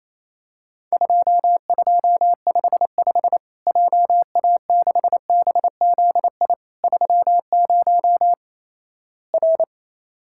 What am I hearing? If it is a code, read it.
Morse code, heard as a keyed tone, about 28 wpm, 700 Hz. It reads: 2255 JA6BZI 30 R
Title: Worked All Europe DX Contest - QTC Exchange - 28wpm